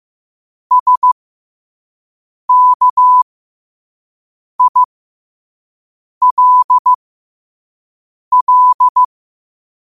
Morse code, heard as a keyed tone, about 15 wpm, spaced as for 5 wpm, 1000 Hz. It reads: SKILL
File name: qotdfiles / 5wpm